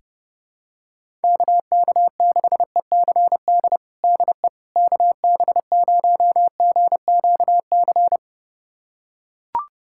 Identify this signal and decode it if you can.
Morse code, heard as a keyed tone, about 30 wpm, 700 Hz. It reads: KK6ECD DE KB0GQC
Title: USA Callsign DE Callsign - 6 Characters - 30wpm